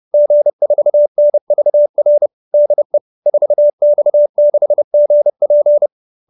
Morse code, heard as a keyed tone, about 30 wpm, 570 Hz. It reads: G4NVR DE 4X6GP